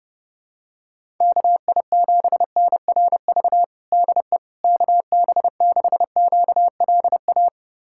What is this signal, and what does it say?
Morse code, heard as a keyed tone, about 30 wpm, 700 Hz. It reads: KI7NRV DE KB6QLA
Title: USA Callsign DE Callsign 30wpm